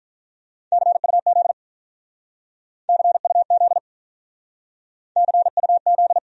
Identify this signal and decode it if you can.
Morse code, heard as a keyed tone, about 43 wpm, 700 Hz. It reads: CUZ CUZ CUZ
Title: Speed-Racing Abbreviations - 35wpm